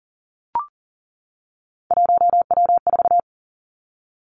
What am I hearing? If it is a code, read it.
Morse code, heard as a keyed tone, about 42 wpm, 700 Hz. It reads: 1W4
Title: Three Challenging Letter-Number Combinations 40wpm